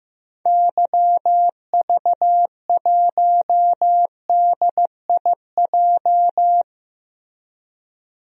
Morse code, heard as a keyed tone, about 15 wpm, 700 Hz. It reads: YV1DIJ
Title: DXCC-100 Call Signs - 6 Characters - 15wpm